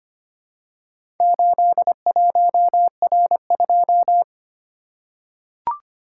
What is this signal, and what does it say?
Morse code, heard as a keyed tone, about 25 wpm, 700 Hz. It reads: 81R2
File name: practice / 25wpm / Four Letter-Number Combinations - 25wpm